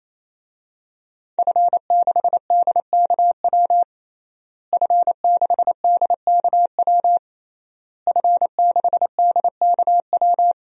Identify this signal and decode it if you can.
Morse code, heard as a keyed tone, about 28 wpm, 700 Hz. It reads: F6DKW F6DKW F6DKW
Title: DXCC-100 Call Signs - 5 Characters - Repeated 3x - 28wpm